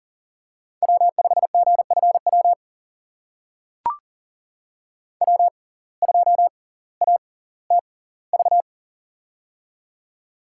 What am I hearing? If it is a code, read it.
Morse code, heard as a keyed tone, about 38 wpm, spaced as for 20 wpm, 700 Hz. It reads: W5GFW W2ATV
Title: USA Call Signs - 5 Characters - 6x Character Spacing - 40wpm